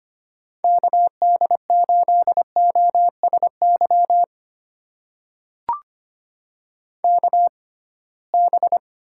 Morse code, heard as a keyed tone, about 25 wpm, spaced as for 19 wpm, 700 Hz. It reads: KD8OSY KB
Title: USA Call Signs - 6 Characters - 6x Character Spacing - 25wpm